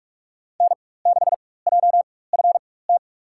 Morse code, heard as a keyed tone, about 46 wpm, spaced as for 21 wpm, 700 Hz. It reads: N6JFT
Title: USA Call Signs - 4x Character Spacing - 45wpm